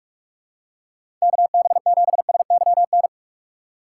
Morse code, heard as a keyed tone, about 46 wpm, 700 Hz. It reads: KB7SYN
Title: USA Call Signs - 4x Character Spacing - 45wpm